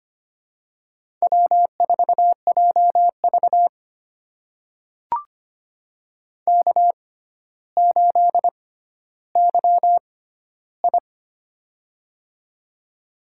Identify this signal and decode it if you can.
Morse code, heard as a keyed tone, about 25 wpm, spaced as for 13 wpm, 700 Hz. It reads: W4JV K8YI
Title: USA Call Signs - 4 Characters - 6x Character Spacing - 25wpm